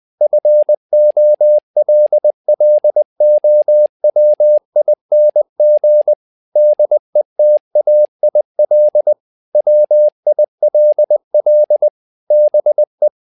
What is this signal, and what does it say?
Morse code, heard as a keyed tone, about 20 wpm, 600 Hz. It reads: FOLLOWING DETAIL WILL BE
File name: tower trouble 20wpm